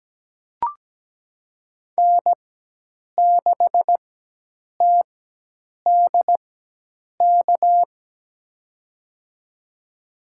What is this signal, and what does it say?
Morse code, heard as a keyed tone, about 17 wpm, spaced as for 8 wpm, 700 Hz. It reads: N6TDK
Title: USA Call Signs - 5 Characters - 4x Character Spacing - 17wpm